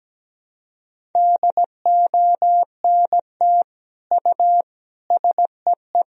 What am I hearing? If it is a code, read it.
Morse code, heard as a keyed tone, about 17 wpm, 700 Hz. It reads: DONT U SEE
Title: Sentences from Top 300 Words and Abbreviations 17wpm